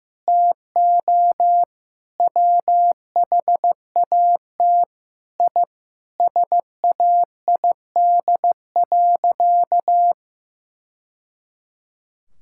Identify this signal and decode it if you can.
Morse code, heard as a keyed tone, about 15 wpm, 700 Hz. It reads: TO WHAT I SAID.
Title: Sentences from Top 1000 Words - Review Part 6 - 15wpm